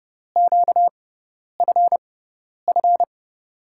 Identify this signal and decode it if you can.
Morse code, heard as a keyed tone, about 30 wpm, spaced as for 10 wpm, 700 Hz. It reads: QFF
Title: USA Call Signs - 6 Characters - 6x Character Spacing - 30wpm